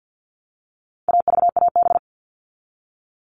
Morse code, heard as a keyed tone, about 50 wpm, 700 Hz. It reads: AVAB
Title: All Cut Numbers - 4-Digit 50wpm